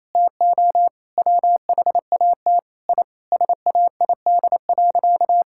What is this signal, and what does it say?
Morse code, heard as a keyed tone, about 27 wpm, 700 Hz. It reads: TO WHAT I SAID.
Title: Sentences from Top 1000 Words - Review Part 6 - 28wpm